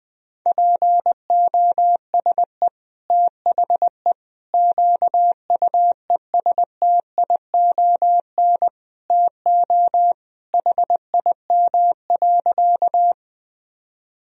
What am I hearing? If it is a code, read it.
Morse code, heard as a keyed tone, about 21 wpm, 700 Hz. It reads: POSE THE QUESTION TO HIM.
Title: Sentences from Top 1000 Words - Review Part 2 - 20wpm